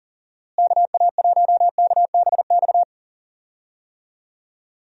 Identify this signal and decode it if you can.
Morse code, heard as a keyed tone, about 38 wpm, 700 Hz. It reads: KA1KBX
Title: USA Call Signs - 6 Characters - Repeated 2x - 40wpm